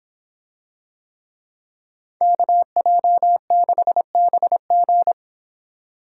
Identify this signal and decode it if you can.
Morse code, heard as a keyed tone, about 24 wpm, 700 Hz. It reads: KJ6BG